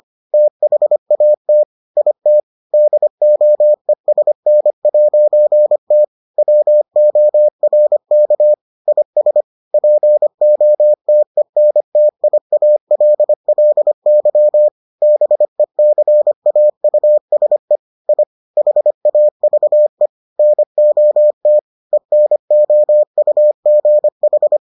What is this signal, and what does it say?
Morse code, heard as a keyed tone, about 25 wpm, 600 Hz. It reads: THAT IT DOESN'T WORK IS POTENTIALLY BECAUSE I HAVE NOT ENOUGH